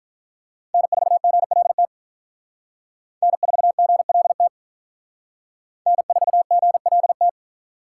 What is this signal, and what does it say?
Morse code, heard as a keyed tone, about 46 wpm, 700 Hz. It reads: N4GLT N4GLT N4GLT